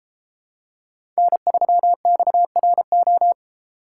Morse code, heard as a keyed tone, about 32 wpm, 700 Hz. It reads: N3XRO